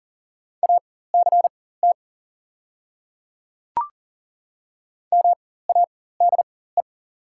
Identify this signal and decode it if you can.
Morse code, heard as a keyed tone, about 40 wpm, spaced as for 19 wpm, 700 Hz. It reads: ACT MADE
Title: 4x Character Spacing - Top 300 Words - 40wpm